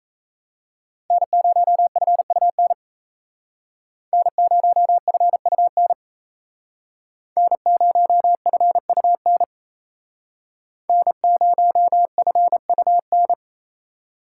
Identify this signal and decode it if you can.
Morse code, heard as a keyed tone, about 35 wpm, 700 Hz. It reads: N0FUN N0FUN N0FUN N0FUN